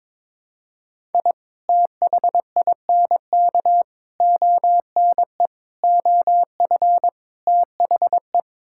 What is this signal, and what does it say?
Morse code, heard as a keyed tone, about 21 wpm, 700 Hz. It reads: I THINK ONE OF THE
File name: practice / 22wpm / Build A Phrase - 5-Word Phrases - 22wpm